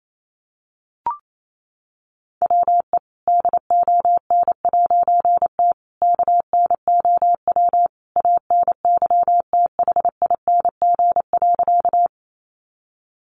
Morse code, heard as a keyed tone, about 28 wpm, 700 Hz. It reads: WE DON'T KNOW ANYTHING.
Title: A Sound Of Thunder 28wpm